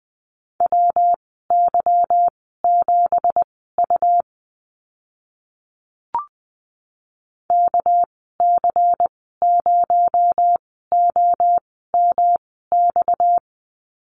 Morse code, heard as a keyed tone, about 20 wpm, spaced as for 14 wpm, 700 Hz. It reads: WY7U KC0OMX
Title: 231 Lesson 37g - Letter x - Call Signs 20wpm